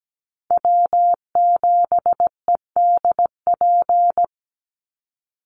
Morse code, heard as a keyed tone, about 17 wpm, 700 Hz. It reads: W7EDP
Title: USA Call Signs 17wpm